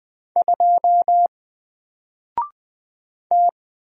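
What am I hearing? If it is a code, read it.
Morse code, heard as a keyed tone, about 20 wpm, 700 Hz. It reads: 2 T
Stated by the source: Single letter and number Morse drill at 20 wpm, from N4LCD.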